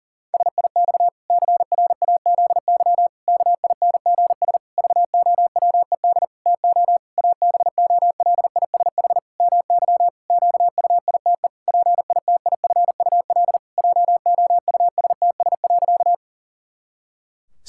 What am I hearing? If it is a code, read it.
Morse code, heard as a keyed tone, about 40 wpm, 700 Hz. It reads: SIX CRAZY KINGS VOWED TO ABOLISH MY QUITE PITIFUL JOUSTS.
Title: Pangram Challenge v2 - 40wpm